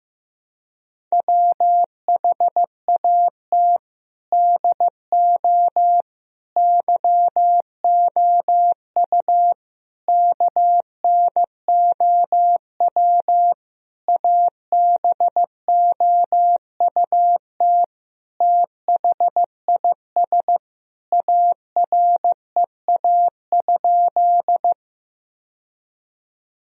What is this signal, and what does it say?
Morse code, heard as a keyed tone, about 16 wpm, 700 Hz. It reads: WHAT DO YOU KNOW ABOUT THIS AREA?
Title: Sentences from Top 1000 Words - Review Part 2 - 15wpm